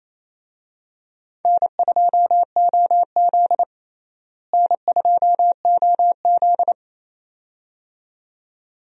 Morse code, heard as a keyed tone, about 28 wpm, 700 Hz. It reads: N2OZ N2OZ
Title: USA Call Signs - 4 Characters - Repeated 2x - 28wpm